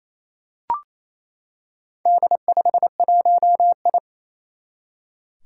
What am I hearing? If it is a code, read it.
Morse code, heard as a keyed tone, about 28 wpm, 700 Hz. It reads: D51I
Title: Four Challenging Letter-Number Combinations 28wpm